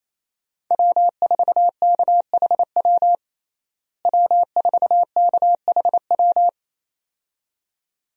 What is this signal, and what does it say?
Morse code, heard as a keyed tone, about 27 wpm, 700 Hz. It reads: W4KHW W4KHW
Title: USA Call Signs - 5 Characters - Repeated 2x - 28wpm